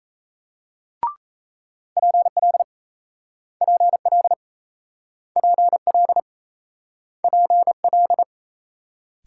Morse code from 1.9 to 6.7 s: PL PL PL

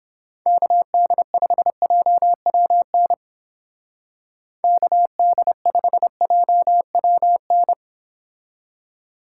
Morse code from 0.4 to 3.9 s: KD5JWN

